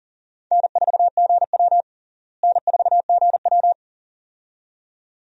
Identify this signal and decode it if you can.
Morse code, heard as a keyed tone, about 37 wpm, 700 Hz. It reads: N4GW N4GW
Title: USA Call Signs - 4 Characters - Repeated 2x - 40wpm